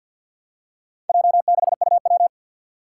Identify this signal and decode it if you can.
Morse code, heard as a keyed tone, about 51 wpm, 700 Hz. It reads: J6UW
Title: Four Challenging Letter-Number Combinations 50wpm